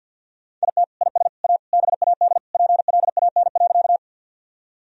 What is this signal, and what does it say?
Morse code, heard as a keyed tone, about 52 wpm, 700 Hz. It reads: IT IS A BAD PLAN.
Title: Sentences from Top 1000 Words - Review Part 2 - 50wpm